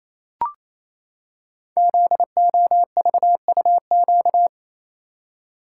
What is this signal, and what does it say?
Morse code, heard as a keyed tone, about 27 wpm, 700 Hz. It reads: ZOVUQ
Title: Code Groups - 5 Letters - 28wpm